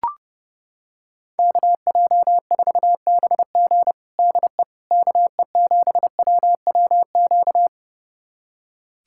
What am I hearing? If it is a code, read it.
Morse code, heard as a keyed tone, about 30 wpm, 700 Hz. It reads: KJ4BG DE KE7WWQ
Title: USA Callsign DE Callsign 30wpm